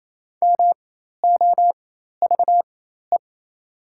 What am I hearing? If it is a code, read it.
Morse code, heard as a keyed tone, about 28 wpm, spaced as for 13 wpm, 700 Hz. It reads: MOVE
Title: 4x Character Spacing - Top 200 Words - 28wpm